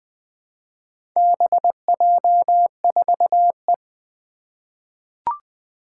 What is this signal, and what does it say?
Morse code, heard as a keyed tone, about 20 wpm, 700 Hz. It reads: BJ4E
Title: Four Challenging Letter-Number Combinations 20wpm